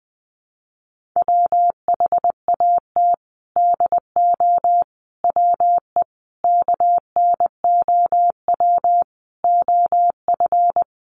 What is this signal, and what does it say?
Morse code, heard as a keyed tone, about 20 wpm, 700 Hz. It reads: WHAT DO WE KNOW OF
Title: Sentences from Top 1400 Words 20wpm